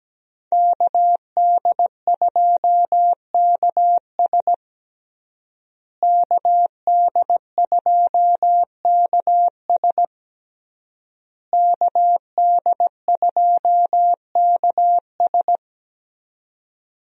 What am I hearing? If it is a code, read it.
Morse code, heard as a keyed tone, about 17 wpm, 700 Hz. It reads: KD2KS KD2KS KD2KS
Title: USA Call Signs - 5 Characters - Repeated 3x - 17wpm